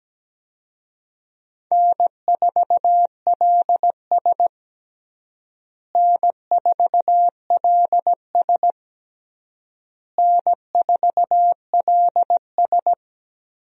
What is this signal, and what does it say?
Morse code, heard as a keyed tone, about 17 wpm, 700 Hz. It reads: N4LS N4LS N4LS
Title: USA Call Signs - 4 Characters - Repeated 3x - 17wpm